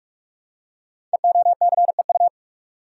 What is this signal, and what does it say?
Morse code, heard as a keyed tone, about 44 wpm, 690 Hz. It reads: EOCEU
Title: Code Groups - 5 Letter:Numbers - 45wpm